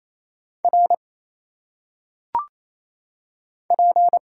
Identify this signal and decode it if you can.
Morse code, heard as a keyed tone, about 27 wpm, 700 Hz. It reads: R P
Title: Single Letters - Phonetic Pronunciation 28wpm